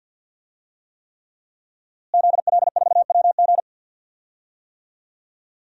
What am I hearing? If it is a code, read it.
Morse code, heard as a keyed tone, about 46 wpm, 700 Hz. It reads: ZL4WG
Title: DXCC-100 Call Signs - 5 Characters - Repeated 3x - 50wpm